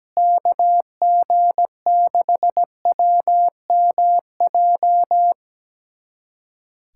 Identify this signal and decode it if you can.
Morse code, heard as a keyed tone, about 17 wpm, 700 Hz. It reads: KG6WMJ
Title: USA Callsign DE Callsign - 6 Characters - 17wpm